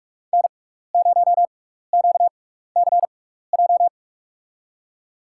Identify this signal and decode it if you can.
Morse code, heard as a keyed tone, about 44 wpm, spaced as for 15 wpm, 700 Hz. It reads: N0QCJ
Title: USA Call Signs - 5 Characters - 6x Character Spacing - 45wpm